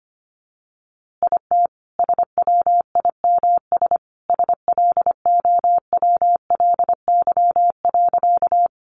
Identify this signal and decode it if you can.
Morse code, heard as a keyed tone, about 25 wpm, 700 Hz. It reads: IT SWIMS SLOWLY.
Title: Easy Phrases 25wpm